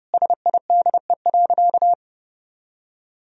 Morse code, heard as a keyed tone, about 32 wpm, 700 Hz. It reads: SIDE.
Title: Sentences from Top 1000 Words - Review Part 1 - 30wpm